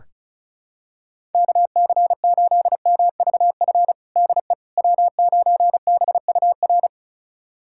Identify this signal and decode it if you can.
Morse code, heard as a keyed tone, about 34 wpm, 700 Hz. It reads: KC8MVF DE W9BUR
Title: USA Callsign DE Callsign 35wpm